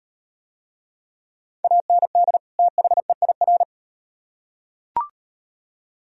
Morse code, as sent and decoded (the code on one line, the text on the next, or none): .- -. -.. / - .... .. .-.
AND THEIR